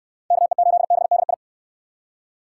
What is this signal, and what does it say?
Morse code, heard as a keyed tone, about 64 wpm, 700 Hz. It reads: KE8BDI